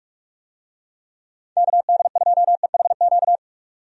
Morse code, heard as a keyed tone, about 46 wpm, 700 Hz. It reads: KD2EHQ
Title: USA Call Signs - 6 Characters - 45wpm